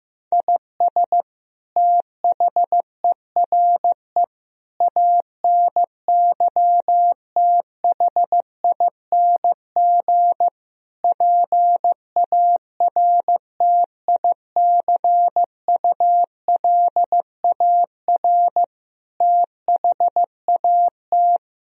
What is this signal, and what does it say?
Morse code, heard as a keyed tone, about 15 wpm, 700 Hz. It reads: IS THERE ANYTHING PARTICULAR THAT
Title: Sentences from Top 1000 Words - Review Part 4 - 15wpm